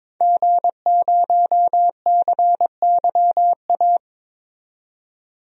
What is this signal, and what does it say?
Morse code, heard as a keyed tone, about 21 wpm, 700 Hz. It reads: G0CYA
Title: DXCC-100 Call Signs - 5 Characters - 22wpm